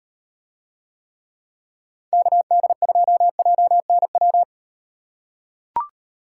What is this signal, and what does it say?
Morse code, heard as a keyed tone, about 38 wpm, 700 Hz. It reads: KD2JNW